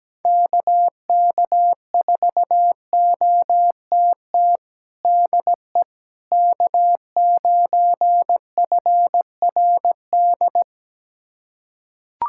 Morse code, heard as a keyed tone, about 17 wpm, 700 Hz. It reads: KK4OTT DE K9FRD
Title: USA Callsign DE Callsign 17wpm